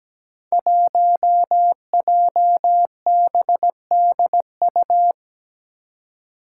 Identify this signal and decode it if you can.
Morse code, heard as a keyed tone, about 17 wpm, 700 Hz. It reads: 1JBDU
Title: Five Challenging Letter-Number Combinations 17wpm